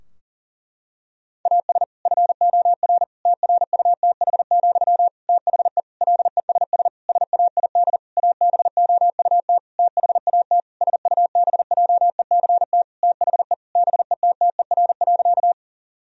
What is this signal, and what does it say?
Morse code, heard as a keyed tone, about 40 wpm, 700 Hz. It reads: AS FOR TRUTH, THE LESS SAID ABOUT THAT SUBJECT THE BETTER.
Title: Sentences from Top 1000 Words - Review Part 2 - 40wpm